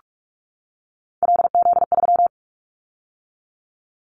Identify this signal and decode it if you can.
Morse code, heard as a keyed tone, about 44 wpm, 700 Hz. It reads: LZ3
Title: Callsign Prefixes for Top 100 DXCC - 45wpm